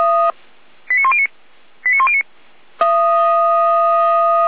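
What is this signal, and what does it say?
Five-tone selective call (ZVEI), heard as quick runs of five tones, tones of about 70 ms. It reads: 89109 89109
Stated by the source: ZVEI five-tone selective call received on SDR, audio.f85360.feuerwehr-orlamuende.org.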